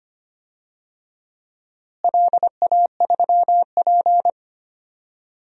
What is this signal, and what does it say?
Morse code, heard as a keyed tone, about 25 wpm, 700 Hz. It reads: LA3P
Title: DXCC-100 Call Signs - 4 Characters - Repeated 3x - 25wpm